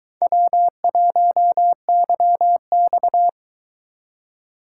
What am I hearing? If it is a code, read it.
Morse code, heard as a keyed tone, about 22 wpm, 700 Hz. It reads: W1YX